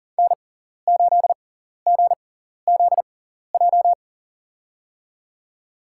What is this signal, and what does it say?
Morse code, heard as a keyed tone, about 40 wpm, spaced as for 14 wpm, 700 Hz. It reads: N8GZJ
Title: USA Call Signs - 5 Characters - 6x Character Spacing - 40wpm